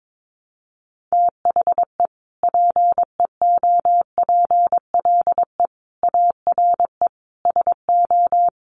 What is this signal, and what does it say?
Morse code, heard as a keyed tone, about 22 wpm, 700 Hz. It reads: THE PEOPLE ARE SO
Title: Additional Easy Phrases 22wpm